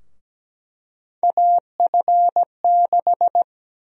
Morse code, heard as a keyed tone, about 17 wpm, 700 Hz. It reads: AF6
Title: USA Call Sign Prefixes - 17wpm